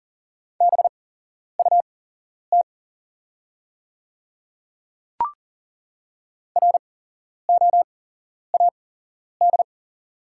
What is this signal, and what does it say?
Morse code, heard as a keyed tone, about 40 wpm, spaced as for 11 wpm, 690 Hz. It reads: BUT ROAD
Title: 8x Character Spacing - Top 500 Words - 40wpm